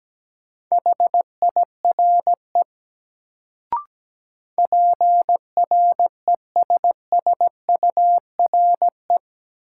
Morse code, heard as a keyed tone, about 17 wpm, 700 Hz. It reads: HIRE PRESSURE
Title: Sentences from Top 1400 Words - New Word Drill Alt - 17wpm